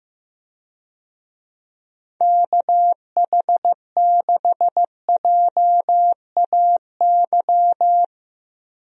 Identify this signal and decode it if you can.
Morse code, heard as a keyed tone, about 15 wpm, 700 Hz. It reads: KH6JAY
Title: DXCC-100 Call Signs - 6 Characters - 15wpm